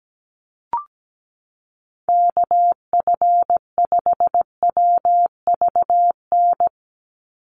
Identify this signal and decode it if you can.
Morse code, heard as a keyed tone, about 17 wpm, 700 Hz. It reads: KF5WVN
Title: USA Call Signs 17wpm